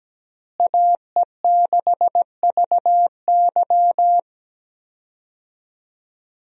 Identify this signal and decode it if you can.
Morse code, heard as a keyed tone, about 17 wpm, 700 Hz. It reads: AE6VY
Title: USA Call Signs - 5 Characters - Repeated 3x - 17wpm